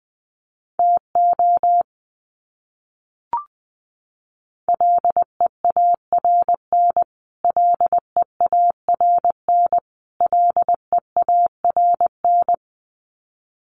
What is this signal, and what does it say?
Morse code, heard as a keyed tone, about 20 wpm, 700 Hz. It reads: TO LEARN LEARN LEARN
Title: Top 500 Words Repeated 20wpm